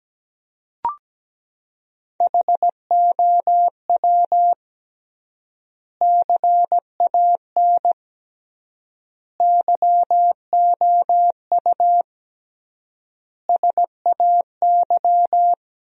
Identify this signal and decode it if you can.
Morse code, heard as a keyed tone, about 17 wpm, 700 Hz. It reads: HOW CAN YOU SAY
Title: 3x Word-Spacing - Sentences from Top 200 Words - 17wpm